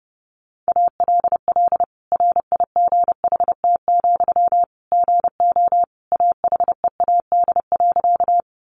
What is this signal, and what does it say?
Morse code, heard as a keyed tone, about 30 wpm, 700 Hz. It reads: ALL RIGHT, GO AHEAD.
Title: Sentences from Top 1100 Words 30wpm